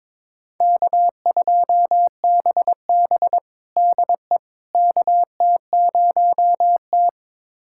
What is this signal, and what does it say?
Morse code, heard as a keyed tone, about 22 wpm, 700 Hz. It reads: K2BB DE KT0T
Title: USA Callsign DE Callsign - 4 Characters - 22wpm